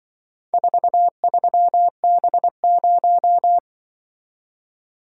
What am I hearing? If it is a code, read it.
Morse code, heard as a keyed tone, about 24 wpm, 700 Hz. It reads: 43B0
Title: Speed-Racing - Four Letter-Number Combinations - 20wpm